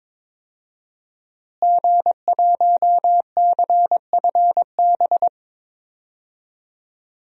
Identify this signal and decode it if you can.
Morse code, heard as a keyed tone, about 22 wpm, 700 Hz. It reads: G1CFB
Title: DXCC-100 Call Signs - 5 Characters - Repeated 2x - 22wpm